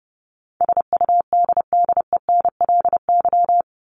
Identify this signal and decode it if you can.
Morse code, heard as a keyed tone, about 30 wpm, 700 Hz. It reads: SUDDENLY
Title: Top 1501-1750 Words 30wpm